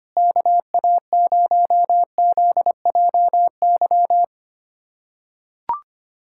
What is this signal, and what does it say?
Morse code, heard as a keyed tone, about 25 wpm, 700 Hz. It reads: KA0ZJY
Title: USA Callsign DE Callsign 25wpm